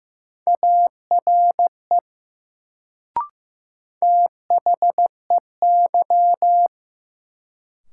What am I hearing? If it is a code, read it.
Morse code, heard as a keyed tone, about 15 wpm, 700 Hz. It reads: ARE THEY
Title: Sentences from Top 100 Words - New Word Drill - 15wpm